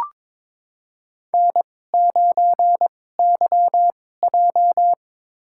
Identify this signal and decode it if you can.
Morse code, heard as a keyed tone, about 23 wpm, spaced as for 16 wpm, 700 Hz. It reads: N9YJ
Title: USA Call Signs - 4 Characters - 2x Character Spacing - 22wpm